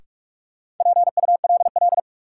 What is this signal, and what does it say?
Morse code, heard as a keyed tone, about 42 wpm, 700 Hz. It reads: PULL
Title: Sentences from Top 400 Words - New Word Drill - 45wpm